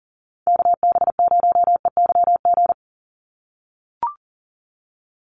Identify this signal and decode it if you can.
Morse code, heard as a keyed tone, about 40 wpm, 700 Hz. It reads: KB0EYG
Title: USA Call Signs 40wpm